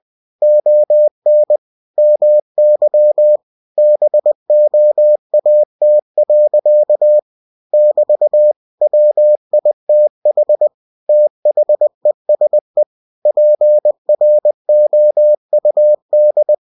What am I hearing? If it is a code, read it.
Morse code, heard as a keyed tone, about 20 wpm, 600 Hz. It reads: ON MY BOAT. <BT> WITH THESE PROUD
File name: cw-books / winnie-the-pooh / winnie-20wpm-page-0098